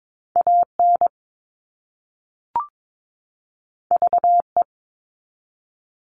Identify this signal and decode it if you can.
Morse code, heard as a keyed tone, about 22 wpm, 700 Hz. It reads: AN VE
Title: All Cut Numbers - 2-Digit 22wpm